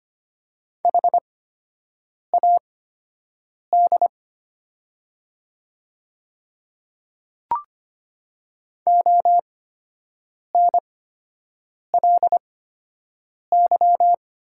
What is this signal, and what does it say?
Morse code, heard as a keyed tone, about 25 wpm, spaced as for 7 wpm, 700 Hz. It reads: HAD ONLY
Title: 8x Character Spacing - Top 200 Words - 25wpm